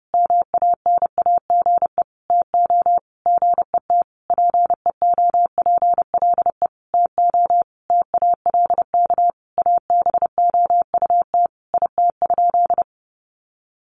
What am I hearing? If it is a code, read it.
Morse code, heard as a keyed tone, about 30 wpm, 700 Hz. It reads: MANAGE TO GET PEOPLE TO TALK ABOUT IT?
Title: Sentences from Top 1400 Words 30wpm